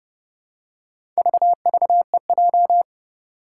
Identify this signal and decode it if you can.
Morse code, heard as a keyed tone, about 30 wpm, 690 Hz. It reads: VVEJ
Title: Four Challenging Letter-Number Combinations 30wpm